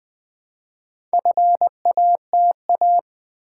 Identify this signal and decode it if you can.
Morse code, heard as a keyed tone, about 20 wpm, 700 Hz. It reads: FATA
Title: Build A Word - 1 Letter At A Time - 5-Letter Words - 20wpm